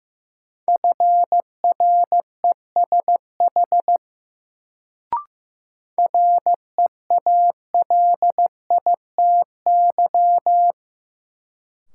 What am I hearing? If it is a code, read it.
Morse code, heard as a keyed tone, about 15 wpm, 700 Hz. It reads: FRESH REALITY
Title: Sentences from Top 1200 Words - New Word Drill - 15wpm